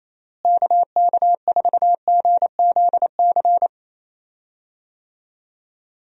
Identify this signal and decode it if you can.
Morse code, heard as a keyed tone, about 28 wpm, 700 Hz. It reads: KK4GZC
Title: USA Call Signs - 6 Characters - Repeated 2x - 28wpm